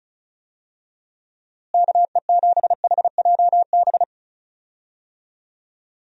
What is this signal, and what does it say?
Morse code, heard as a keyed tone, about 34 wpm, 700 Hz. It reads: KE7HJB
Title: USA Call Signs - 6 Characters - Repeated 2x - 35wpm